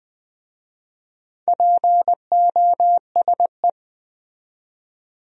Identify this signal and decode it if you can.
Morse code, heard as a keyed tone, about 22 wpm, 700 Hz. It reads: POSE